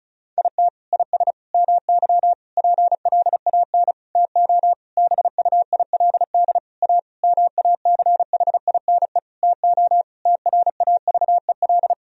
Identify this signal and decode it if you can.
Morse code, heard as a keyed tone, about 34 wpm, 700 Hz. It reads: IT IS MY PLAN TO BUILD A MACHINE TO TRAVEL
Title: Sentences from Top 1000 Words - Review Part 2 - 35wpm